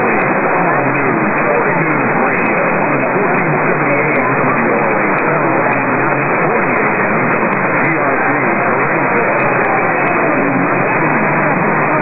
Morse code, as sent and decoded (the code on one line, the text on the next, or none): none